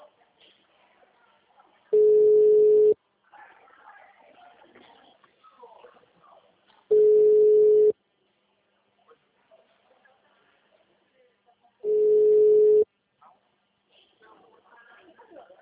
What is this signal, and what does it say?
Ringback tone (Europe, 425 Hz), with speech.
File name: out- -2013- - - .82496